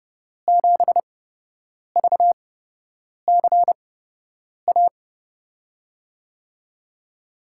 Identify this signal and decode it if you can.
Morse code, heard as a keyed tone, about 30 wpm, spaced as for 8 wpm, 700 Hz. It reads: 7VCA
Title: USA Call Signs - 8x Character Spacing - 30wpm